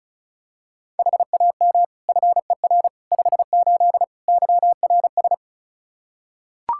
Morse code, heard as a keyed tone, about 36 wpm, 700 Hz. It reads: HAM FER 58 YRS
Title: Practice QSO - Exchange 3 - More Info - 35wpm